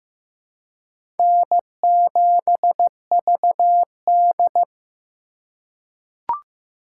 Morse code, heard as a keyed tone, about 15 wpm, 700 Hz. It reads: N7VD
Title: USA Call Signs - 4 Characters - 4x Character Spacing - 15wpm